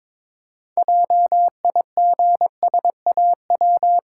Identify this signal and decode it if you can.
Morse code, heard as a keyed tone, about 23 wpm, 690 Hz. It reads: JIGSAW
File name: Difficult Words with Hints - 22wpm